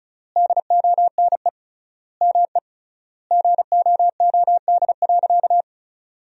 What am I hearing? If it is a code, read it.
Morse code, heard as a keyed tone, about 34 wpm, 700 Hz. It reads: DONE ME GOOD.
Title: 3x Word-Spacing - Sentences from Top 500 Words - 35wpm